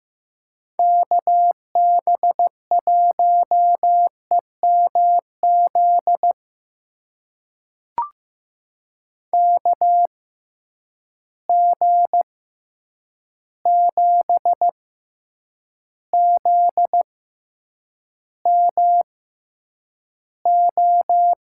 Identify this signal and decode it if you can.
Morse code, heard as a keyed tone, about 15 wpm, spaced as for 8 wpm, 700 Hz. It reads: KB1EMZ KG7ZMO
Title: USA Call Signs - 6 Characters - 6x Character Spacing - 15wpm